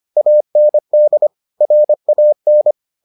Morse code, heard as a keyed tone, about 25 wpm, 600 Hz. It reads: AND RAN